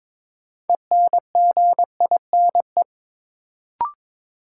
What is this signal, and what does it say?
Morse code, heard as a keyed tone, about 21 wpm, 700 Hz. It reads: ENGINE
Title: Sentences from Top 1500 Words - New Word Drill Alt - 22wpm